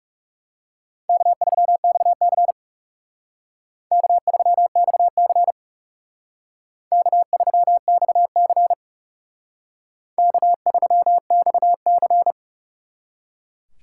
Morse code, heard as a keyed tone, about 39 wpm, 700 Hz. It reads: K3XC K3XC K3XC K3XC